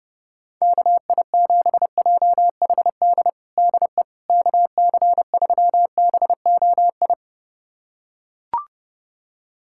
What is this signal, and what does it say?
Morse code, heard as a keyed tone, about 30 wpm, 700 Hz. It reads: KI7JHD DE KC3BOI
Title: USA Callsign DE Callsign 30wpm